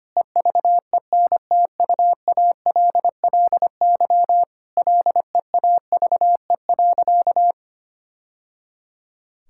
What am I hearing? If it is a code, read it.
Morse code, heard as a keyed tone, about 25 wpm, 700 Hz. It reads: EVENTUALLY LEAVE.
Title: Sentences from Top 1000 Words - Review Part 6 - 25wpm